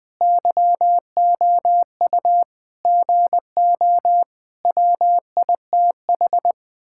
Morse code, heard as a keyed tone, about 20 wpm, 700 Hz. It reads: YOU GO WITH
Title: Sentences from Top 1000 Words - Review Part 6 - 20wpm